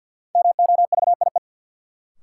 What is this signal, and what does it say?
Morse code, heard as a keyed tone, about 51 wpm, 700 Hz. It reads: MOVIE
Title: Sentences from Top 1000 Words - Review Part 5 - New Word Drill - 50wpm